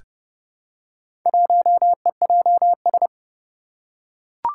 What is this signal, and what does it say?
Morse code, heard as a keyed tone, about 30 wpm, 700 Hz. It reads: 1EJS
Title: Four Challenging Letter-Number Combinations 30wpm